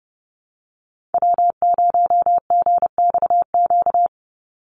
Morse code, heard as a keyed tone, about 30 wpm, 700 Hz. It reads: W0GXQ
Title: Call Signs Repeated 3x - 30wpm